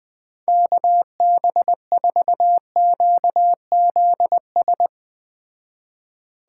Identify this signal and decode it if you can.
Morse code, heard as a keyed tone, about 20 wpm, 700 Hz. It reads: KB4QZS